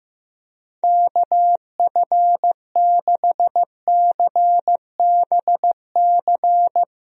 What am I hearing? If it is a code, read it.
Morse code, heard as a keyed tone, about 15 wpm, 700 Hz. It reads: KF6CBC
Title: USA Call Signs - 6 Characters - Repeated 2x - 15wpm